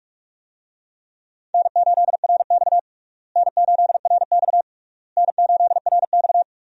Morse code, heard as a keyed tone, about 42 wpm, 700 Hz. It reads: N8RX N8RX N8RX
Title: USA Call Signs - 4 Characters - Repeated 3x - 45wpm